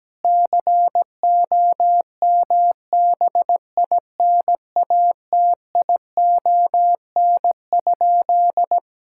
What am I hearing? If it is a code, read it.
Morse code, heard as a keyed tone, about 17 wpm, 700 Hz. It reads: COMBINATION?
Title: Sentences from Top 1000 Words - Review Part 6 - 17wpm